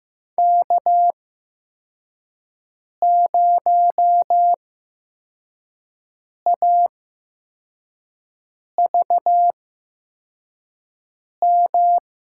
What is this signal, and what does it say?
Morse code, heard as a keyed tone, about 15 wpm, spaced as for 4 wpm, 700 Hz. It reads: K0AVM
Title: USA Call Signs - 8x Character Spacing - 15wpm